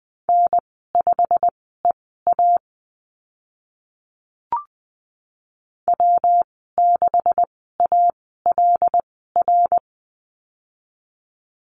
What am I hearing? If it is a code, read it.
Morse code, heard as a keyed tone, about 20 wpm, spaced as for 14 wpm, 700 Hz. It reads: N5EA W6ALR
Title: 154 Lesson 25g - Letter p - Call Signs 20wpm